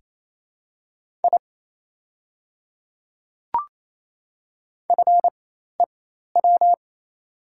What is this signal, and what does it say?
Morse code, heard as a keyed tone, about 28 wpm, spaced as for 13 wpm, 700 Hz. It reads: I FEW
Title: 4x Character Spacing - Top 300 Words - 28wpm